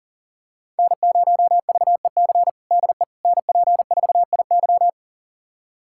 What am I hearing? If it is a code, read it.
Morse code, heard as a keyed tone, about 40 wpm, 700 Hz. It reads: N0VEC DE NP4IY
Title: USA Callsign DE Callsign - 5 Characters - 40wpm